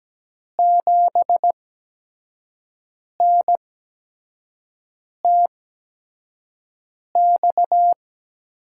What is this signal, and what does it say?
Morse code, heard as a keyed tone, about 17 wpm, spaced as for 5 wpm, 700 Hz. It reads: 7NTX
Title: USA Call Signs - 6 Characters - 8x Character Spacing - 17wpm